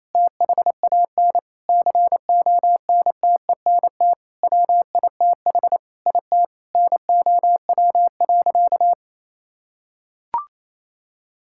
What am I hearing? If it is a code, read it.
Morse code, heard as a keyed tone, about 27 wpm, 700 Hz. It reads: THAN CONTENT WITH IT NOW.
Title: Sentences from Top 1000 Words - Review Part 6 - 28wpm